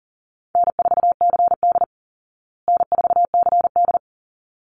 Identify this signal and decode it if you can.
Morse code, heard as a keyed tone, about 40 wpm, 700 Hz. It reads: N4CD N4CD
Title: Call Signs Repeated 3x - 40wpm